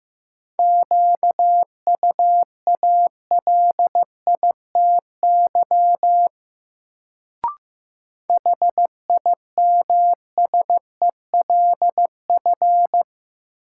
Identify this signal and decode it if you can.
Morse code, heard as a keyed tone, about 15 wpm, 700 Hz. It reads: QUALITY HIMSELF
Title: Sentences from Top 1500 Words - New Word Drill Alt - 15wpm